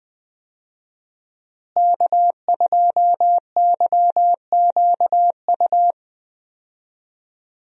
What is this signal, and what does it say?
Morse code, heard as a keyed tone, about 20 wpm, 700 Hz. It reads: K2YQU
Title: USA Call Signs - 5 Characters - Repeated 3x - 20wpm